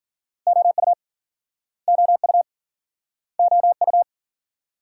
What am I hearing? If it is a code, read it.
Morse code, heard as a keyed tone, about 48 wpm, 700 Hz. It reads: OU OU OU
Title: Speed-Racing Multi-letter Phonemes - 40wpm